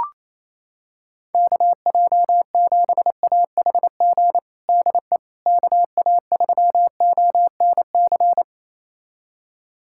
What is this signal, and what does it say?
Morse code, heard as a keyed tone, about 28 wpm, 700 Hz. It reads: KJ7AHG DE KA3ONC
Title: USA Callsign DE Callsign 28wpm